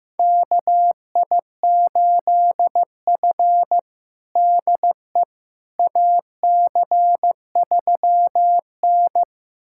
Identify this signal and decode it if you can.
Morse code, heard as a keyed tone, about 15 wpm, 700 Hz. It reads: KI8F DE AC3N
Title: USA Callsign DE Callsign - 4 Characters - 15wpm